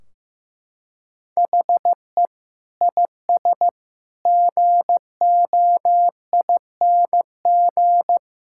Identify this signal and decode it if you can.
Morse code, heard as a keyed tone, about 15 wpm, 700 Hz. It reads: HE IS GOING
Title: Sentences from Top 1600 Words 15wpm